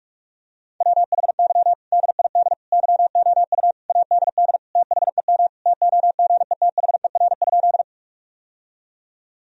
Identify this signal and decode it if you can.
Morse code, heard as a keyed tone, about 45 wpm, 700 Hz. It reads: WHY DID YOU ADD THEM TOGETHER?
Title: Sentences from Top 1000 Words - Review Part 1 - 45wpm